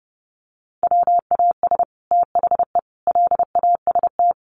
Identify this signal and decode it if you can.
Morse code, heard as a keyed tone, about 28 wpm, 700 Hz. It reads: WAS THE LAST
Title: Sets of 5 Words Encore2 30wpm